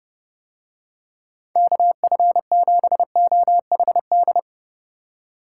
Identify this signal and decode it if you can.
Morse code, heard as a keyed tone, about 29 wpm, 700 Hz. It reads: KF7OHD